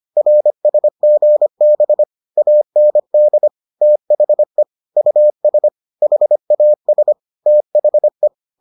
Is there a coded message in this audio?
Yes: Morse code from 0.2 to 8.3 s, heard as a keyed tone, about 25 wpm, 600 Hz. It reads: RSGB AND THE US HAS THE